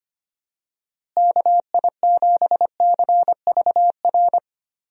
Morse code, heard as a keyed tone, about 25 wpm, 700 Hz. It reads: KI7CVR
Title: USA Call Signs - 6 Characters - Repeated 2x - 25wpm